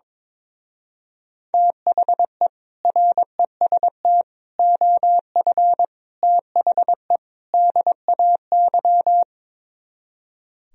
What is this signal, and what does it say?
Morse code, heard as a keyed tone, about 23 wpm, 700 Hz. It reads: THE REST OF THE DAY
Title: Speed-Racing - Sets of 5 Words 22wpm